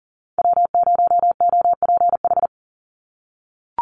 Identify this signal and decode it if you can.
Morse code, heard as a keyed tone, about 40 wpm, 700 Hz. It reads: W0OPH
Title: USA Call Signs 40wpm